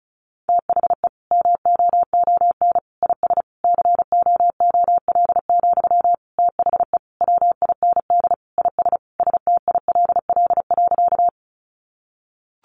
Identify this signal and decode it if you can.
Morse code, heard as a keyed tone, about 34 wpm, 700 Hz. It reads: THE MOON IS COOL, THE WIND IS STILL.
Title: Sentences from Top 1200 Words 35wpm